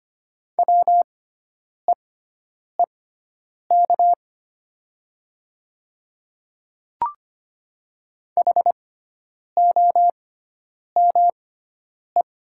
Morse code, heard as a keyed tone, about 25 wpm, spaced as for 9 wpm, 690 Hz. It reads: WEEK HOME
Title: 6x Character Spacing - Top 500 Words - 25wpm